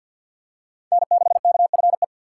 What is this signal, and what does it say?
Morse code, heard as a keyed tone, about 50 wpm, 700 Hz. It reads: N6KFE
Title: USA Call Signs - 5 Characters - 4x Character Spacing - 50wpm